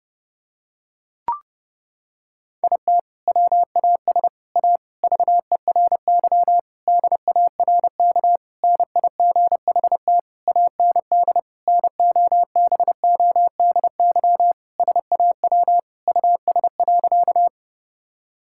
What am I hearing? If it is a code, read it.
Morse code, heard as a keyed tone, about 32 wpm, 700 Hz. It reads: IT WAS A VERY DARK NIGHT AND NOBODY SAW US.
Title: Sentences from Top 1600 Words 30wpm